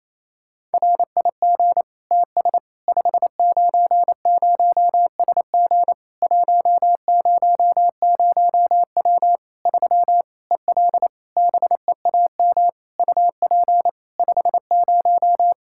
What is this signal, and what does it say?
Morse code, heard as a keyed tone, about 27 wpm, 700 Hz. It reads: RIG TS 590SG 100W 3 EL BEAM UP 50
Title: K1USN SST QSO Practice 28wpm